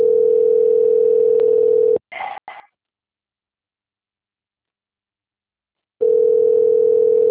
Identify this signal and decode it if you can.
Ringback tone (North America), 440 + 480 Hz.